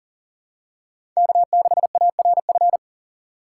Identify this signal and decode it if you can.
Morse code, heard as a keyed tone, about 40 wpm, 700 Hz. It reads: K6ARF
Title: USA Call Signs - 6x Character Spacing - 40wpm